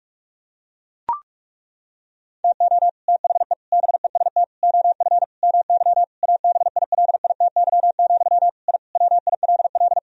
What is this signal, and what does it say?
Morse code, heard as a keyed tone, about 41 wpm, 700 Hz. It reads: TO THE BEST OF MY ABILITY, I WILL
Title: Sentences from Top 1000 Words - Review Part 3 - 45wpm